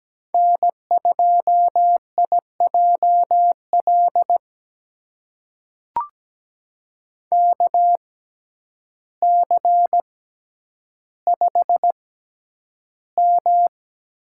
Morse code, heard as a keyed tone, about 17 wpm, spaced as for 9 wpm, 700 Hz. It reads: N2IJL KC5M
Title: USA Call Signs - 6x Character Spacing - 17wpm